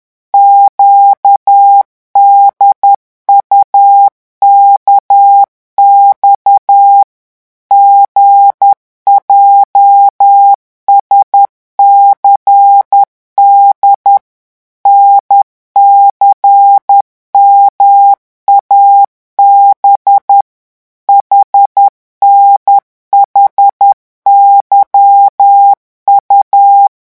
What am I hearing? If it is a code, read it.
Morse code, heard as a keyed tone, about 10 wpm, 800 Hz. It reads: QDUKX GJSCD NCMAB HNHYU